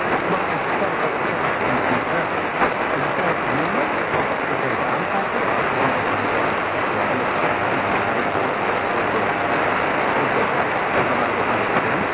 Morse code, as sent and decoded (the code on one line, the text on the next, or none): none